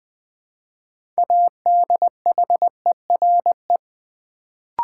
Morse code, heard as a keyed tone, about 20 wpm, 700 Hz. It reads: ADHERE
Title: Build A Word - 2 Letters At A Time - 6-Letter Words - 20wpm